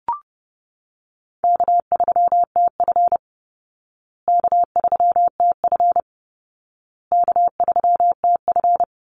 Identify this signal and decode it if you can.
Morse code, heard as a keyed tone, about 30 wpm, 700 Hz. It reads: K3TF K3TF K3TF
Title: Call Signs Repeated 2x - 30wpm